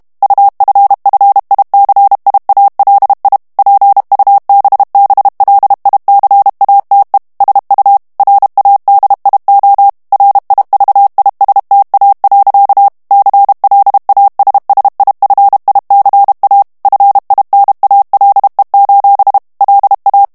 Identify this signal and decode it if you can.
Morse code, heard as a keyed tone, about 32 wpm, 800 Hz. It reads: UFFICIALI PUBBLICATE SU RADIO RIVISTA. CLASSIFICA FINALE: LA
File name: ITALIEN texte clair 2 vitesse 30 mots minute